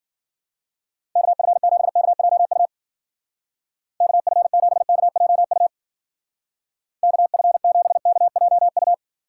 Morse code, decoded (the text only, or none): KF7KJU KF7KJU KF7KJU